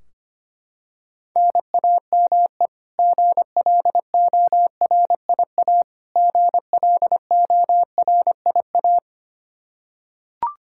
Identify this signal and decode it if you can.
Morse code, heard as a keyed tone, about 25 wpm, 700 Hz. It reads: NAME GLORIA GLORIA
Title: K1USN SST QSO Practice 25wpm